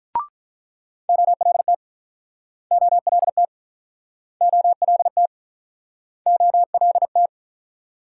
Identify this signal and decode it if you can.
Morse code, heard as a keyed tone, about 48 wpm, 700 Hz. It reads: OLT OLT OLT OLT